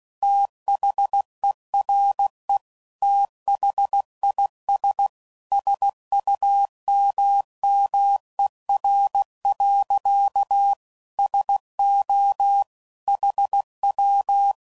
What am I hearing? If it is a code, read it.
Morse code, heard as a keyed tone, about 16 wpm, 800 Hz. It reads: THERE THIS SUMMER. SO HW
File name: CW Code Groups @15WPM / QSOs / WD0FBT